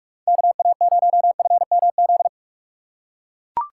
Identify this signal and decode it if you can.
Morse code, heard as a keyed tone, about 42 wpm, 700 Hz. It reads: KA0FMZ
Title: USA Call Signs - 2x Character Spacing - 45wpm